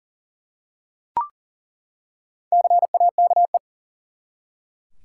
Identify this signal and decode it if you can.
Morse code, heard as a keyed tone, about 38 wpm, 700 Hz. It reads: CAKE
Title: Four Letter Words - 40wpm